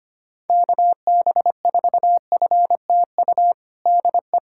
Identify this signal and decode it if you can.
Morse code, heard as a keyed tone, about 25 wpm, 700 Hz. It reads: KB4FTU DE
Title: USA Callsign DE Callsign - 6 Characters - 25wpm